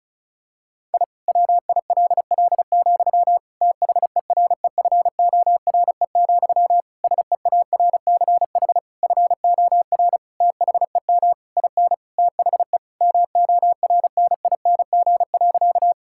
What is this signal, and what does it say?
Morse code, heard as a keyed tone, about 36 wpm, 700 Hz. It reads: I WILL, THEREFORE, SEARCH FOR THEM IN THE MORNING.
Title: Sentences from Top 1000 Words - Review Part 5 - 35wpm